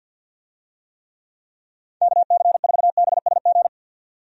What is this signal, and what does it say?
Morse code, heard as a keyed tone, about 48 wpm, 700 Hz. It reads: KC4BSG
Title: USA Call Signs - 6 Characters - 50wpm